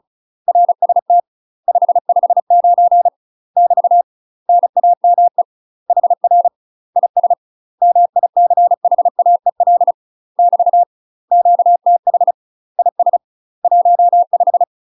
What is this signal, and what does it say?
Morse code, heard as a keyed tone, about 36 wpm, 700 Hz. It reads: RST 559 <BT> NAME HR IS MICHAEL <BT> QTH IS 15
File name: UK MASTER QSOs TRAINING / 35WPM / UK MASTER QSOs 0027